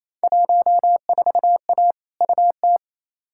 Morse code, heard as a keyed tone, about 28 wpm, 700 Hz. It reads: 14A UT
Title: Field Day Exchanges Encore - 28wpm